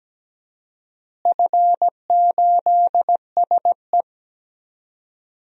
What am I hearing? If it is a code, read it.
Morse code, heard as a keyed tone, about 17 wpm, 700 Hz. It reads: F8SE
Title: DXCC-100 Call Signs - 4 Characters - Repeated 2x - 17wpm